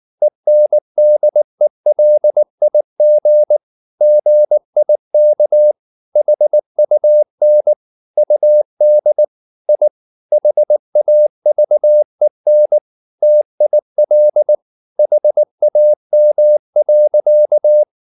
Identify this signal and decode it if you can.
Morse code, heard as a keyed tone, about 19 wpm, 600 Hz. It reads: ENDELIG GIK HUN UD I HAVEN TIL HAM.
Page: Ferie 19 wpm